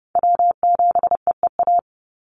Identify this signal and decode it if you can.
Morse code, heard as a keyed tone, about 30 wpm, 700 Hz. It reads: W7EEA
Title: USA Call Signs 30wpm